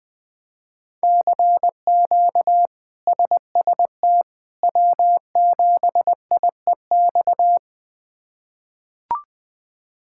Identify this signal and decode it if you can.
Morse code, heard as a keyed tone, about 20 wpm, 700 Hz. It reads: CQ SST W7IEX
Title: K1USN SST QSO Practice - 20wpm